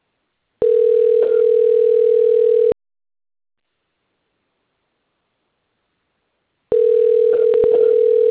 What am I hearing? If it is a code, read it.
Ringback tone (North America), 440 + 480 Hz.